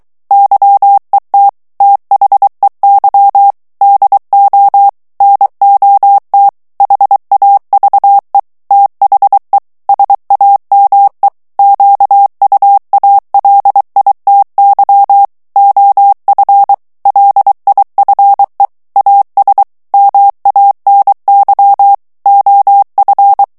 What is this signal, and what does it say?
Morse code, heard as a keyed tone, about 23 wpm, 800 Hz. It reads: YET THEY DO NOT HAVE THE SAME QUALITY OF LIFE AS MANY OF
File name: morse mp3 / ANGLAIS texte clair 1 vitesse 22 mots minute